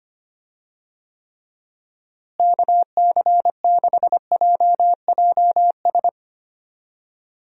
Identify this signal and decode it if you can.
Morse code, heard as a keyed tone, about 25 wpm, 700 Hz. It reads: KC6JJS
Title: USA Call Signs - 6 Characters - Repeated 2x - 25wpm